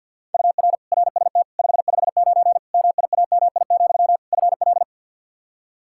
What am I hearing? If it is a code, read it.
Morse code, heard as a keyed tone, about 52 wpm, 700 Hz. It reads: UR RST 559 MIAMI, FL
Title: K1USN SST QSO Practice 50wpm